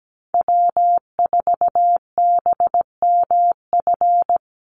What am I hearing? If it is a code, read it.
Morse code, heard as a keyed tone, about 17 wpm, 700 Hz. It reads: W4BMF
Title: USA Call Signs 17wpm